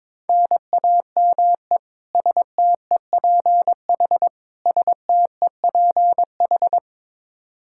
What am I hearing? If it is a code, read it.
Morse code, heard as a keyed tone, about 22 wpm, 700 Hz. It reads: NAME STEPH STEPH
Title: Practice QSO - Exchange 1 - Beginning - 22wpm